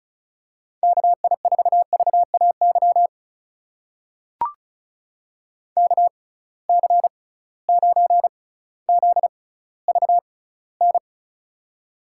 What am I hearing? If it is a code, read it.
Morse code, heard as a keyed tone, about 36 wpm, spaced as for 18 wpm, 700 Hz. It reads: KI4VAY KC9ZVN
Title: USA Call Signs - 6x Character Spacing - 35wpm